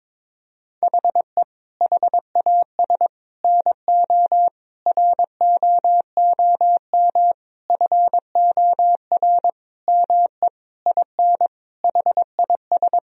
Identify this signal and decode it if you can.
Morse code, heard as a keyed tone, about 22 wpm, 700 Hz. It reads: HE HAS NO ROOM FOR ME IN HIS
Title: Sentences from Top 1000 Words - Review Part 4 - 22wpm